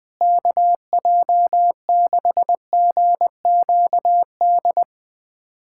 Morse code, heard as a keyed tone, about 20 wpm, 700 Hz. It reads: KJ6GQD
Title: USA Callsign DE Callsign 20wpm